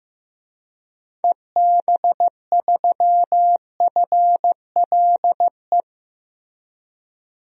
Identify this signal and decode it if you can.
Morse code, heard as a keyed tone, about 15 wpm, 700 Hz. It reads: EB3FLE
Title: DXCC-100 Call Signs - 6 Characters - Repeated 2x - 15wpm